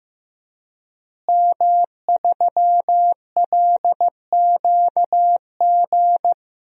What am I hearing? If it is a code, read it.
Morse code, heard as a keyed tone, about 15 wpm, 700 Hz. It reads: M3LQG
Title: DXCC-100 Call Signs - 5 Characters - Repeated 2x - 15wpm